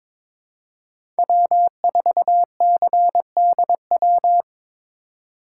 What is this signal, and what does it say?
Morse code, heard as a keyed tone, about 22 wpm, 700 Hz. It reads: W4CDW
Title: USA Call Signs - 5 Characters - 4x Character Spacing - 22wpm